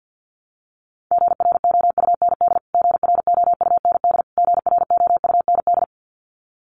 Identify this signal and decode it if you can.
Morse code, heard as a keyed tone, about 50 wpm, 700 Hz. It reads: GROUND GROUND GROUND
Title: Top 500 Words Repeated 50wpm